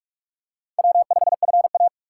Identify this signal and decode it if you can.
Morse code, heard as a keyed tone, about 42 wpm, 700 Hz. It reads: W5FA